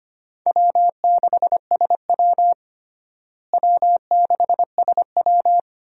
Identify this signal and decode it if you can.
Morse code, heard as a keyed tone, about 25 wpm, 700 Hz. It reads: W6SW W6SW
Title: USA Call Signs - 4 Characters - Repeated 2x - 25wpm